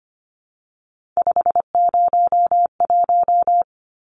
Morse code, heard as a keyed tone, about 25 wpm, 690 Hz. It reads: 501
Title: Numbers - 3-Digit 25wpm